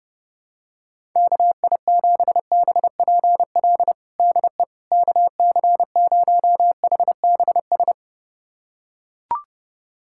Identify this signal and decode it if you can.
Morse code, heard as a keyed tone, about 30 wpm, 700 Hz. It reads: KI7BPL DE KC0HBS
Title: USA Callsign DE Callsign - 6 Characters - 30wpm